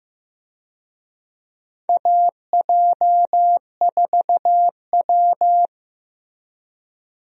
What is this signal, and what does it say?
Morse code, heard as a keyed tone, about 15 wpm, 700 Hz. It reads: AJ4W
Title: USA Call Signs - 4 Characters - Repeated 2x - 15wpm